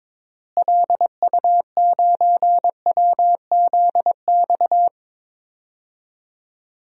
Morse code, heard as a keyed tone, about 22 wpm, 700 Hz. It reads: LU9WZX
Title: DXCC-100 Call Signs - 6 Characters - Repeated 3x - 22wpm